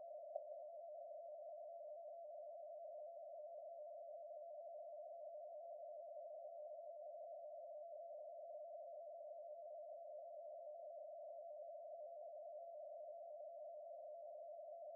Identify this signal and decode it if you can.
Two-tone teleprinter signal, about 80 Hz shift.